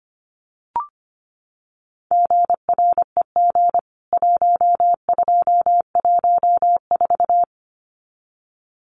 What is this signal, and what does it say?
Morse code, heard as a keyed tone, about 25 wpm, 700 Hz. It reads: GREG 1214
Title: CWT Exchange 25wpm